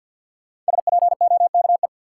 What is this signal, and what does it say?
Morse code, heard as a keyed tone, about 53 wpm, 700 Hz. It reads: SPOKE
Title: Sentences from Top 1200 Words - New Word Drill - 50wpm